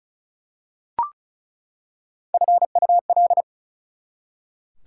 Morse code, heard as a keyed tone, about 34 wpm, 700 Hz. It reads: FUL
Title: Prefixes and Suffixes 35wpm